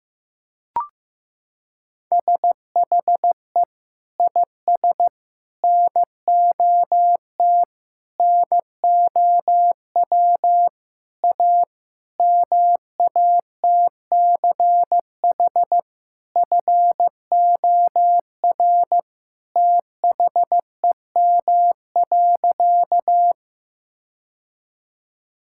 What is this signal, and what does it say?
Morse code, heard as a keyed tone, about 15 wpm, 700 Hz. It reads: SHE IS NOT NOW A MATCH FOR THEM.
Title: Sentences from Top 1600 Words 15wpm